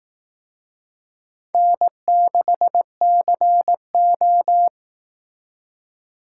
Morse code, heard as a keyed tone, about 19 wpm, 700 Hz. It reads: N6CO